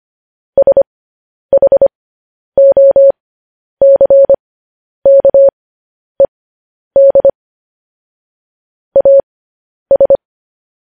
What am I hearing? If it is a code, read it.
Morse code, heard as a keyed tone, about 25 wpm, spaced as for 10 wpm, 550 Hz. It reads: SHOCKED AS
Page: news-headlines-in-morse-code-at-10-wpm